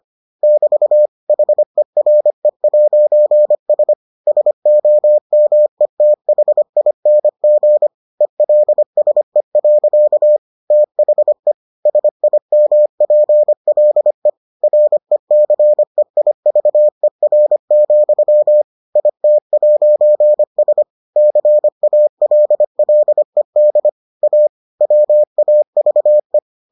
Morse code, heard as a keyed tone, about 25 wpm, 600 Hz. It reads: <BT> HERE'S SOMETHING ELSE. THE SIMPLE RECEIVER, IT'S CALLED A WAVE